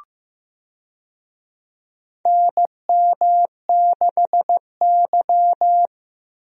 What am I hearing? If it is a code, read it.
Morse code, heard as a keyed tone, about 15 wpm, 700 Hz. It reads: NM6Y
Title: USA Call Signs - 4 Characters - 15wpm